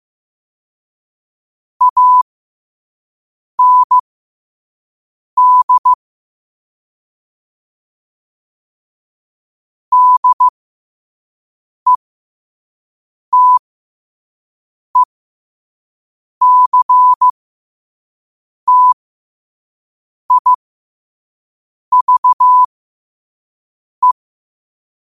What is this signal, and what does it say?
Morse code, heard as a keyed tone, about 15 wpm, spaced as for 5 wpm, 1000 Hz. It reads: AND DETECTIVE
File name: qotdfiles / 5wpm